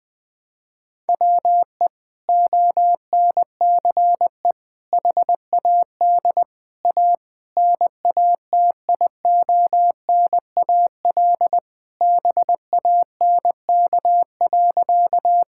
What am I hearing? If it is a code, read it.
Morse code, heard as a keyed tone, about 21 wpm, 700 Hz. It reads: WE ONCE HAD A NATIONAL BANK.
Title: Sentences from Top 1000 Words - Review Part 6 - 20wpm